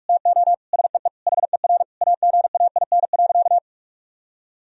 Morse code, heard as a keyed tone, about 46 wpm, 700 Hz. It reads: TO SEE HER AGAIN.
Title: Sentences from Top 1000 Words - Review Part 4 - 45wpm